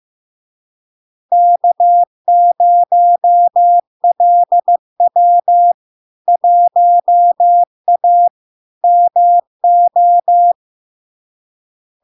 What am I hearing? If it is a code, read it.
Morse code, heard as a keyed tone, about 15 wpm, 700 Hz. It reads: K0LW 1A MO
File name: LARK-FD-2022-Log-in-CW-15wpm-700-Hz